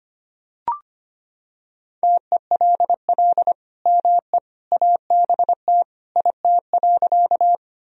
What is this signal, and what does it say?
Morse code, heard as a keyed tone, about 25 wpm, 700 Hz. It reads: TELL ME ABT IT.
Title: Sentences from Top 200 Words and Abbreviations 25wpm